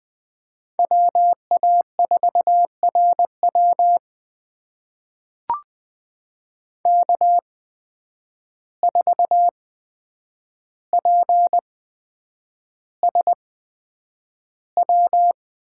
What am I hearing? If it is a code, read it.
Morse code, heard as a keyed tone, about 20 wpm, spaced as for 9 wpm, 700 Hz. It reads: WA4RW K4PSW
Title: USA Call Signs - 5 Characters - 8x Character Spacing - 20wpm